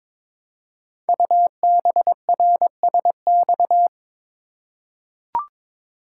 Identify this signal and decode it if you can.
Morse code, heard as a keyed tone, about 22 wpm, 700 Hz. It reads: UBRSX
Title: Code Groups - 5 Letters - 22wpm